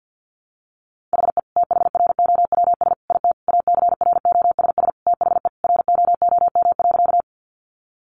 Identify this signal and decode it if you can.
Morse code, heard as a keyed tone, about 50 wpm, 700 Hz. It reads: HE THROWS IT ACROSS THE ROOM.
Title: Additional Easy Phrases 50wpm